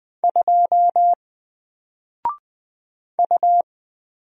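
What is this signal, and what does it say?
Morse code, heard as a keyed tone, about 20 wpm, 700 Hz. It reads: 2 U
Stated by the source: Single letter and number Morse drill at 20 wpm, from N4LCD.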